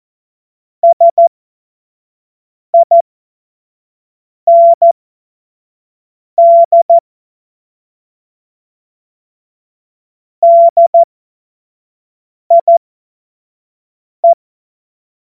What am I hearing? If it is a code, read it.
Morse code, heard as a keyed tone, about 14 wpm, spaced as for 5 wpm, 680 Hz. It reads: SIND DIE
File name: Witz-23-September-2025-25-F70